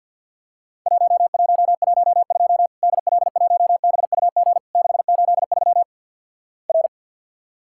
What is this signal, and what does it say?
Morse code, heard as a keyed tone, about 50 wpm, 700 Hz. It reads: 1112 DL1BUG 683 R
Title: Worked All Europe DX Contest - QTC Exchange - 50wpm